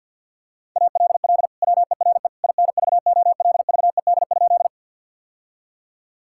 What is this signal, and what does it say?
Morse code, heard as a keyed tone, about 52 wpm, 700 Hz. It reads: ALL WERE INVOLVED?
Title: Sentences from Top 1000 Words - Review Part 5 - 50wpm